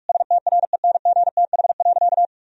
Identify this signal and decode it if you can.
Morse code, heard as a keyed tone, about 46 wpm, 700 Hz. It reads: STRENGTH.
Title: Sentences from Top 1000 Words - Review Part 4 - 45wpm